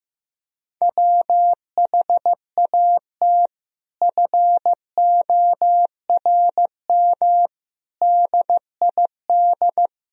Morse code, heard as a keyed tone, about 15 wpm, 700 Hz. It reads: WHAT FORM DID
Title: Sentences from Top 1000 Words - Review Part 1 - 15wpm